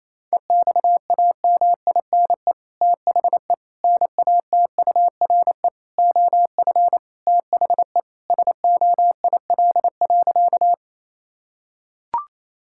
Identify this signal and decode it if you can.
Morse code, heard as a keyed tone, about 28 wpm, 700 Hz. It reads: EXAMINE THE NATURE OF THE SOIL.
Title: Sentences from Top 1600 Words 28wpm